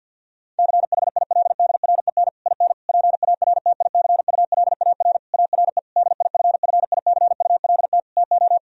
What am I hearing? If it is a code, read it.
Morse code, heard as a keyed tone, about 50 wpm, 700 Hz. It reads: CHILDREN IN PARTICULAR ARE DIFFICULT TO
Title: Sentences from Top 1000 Words - Review Part 4 - 50wpm